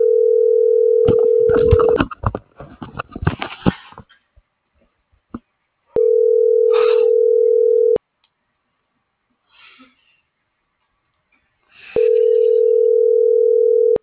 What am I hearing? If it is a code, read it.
Ringback tone (North America), 440 + 480 Hz.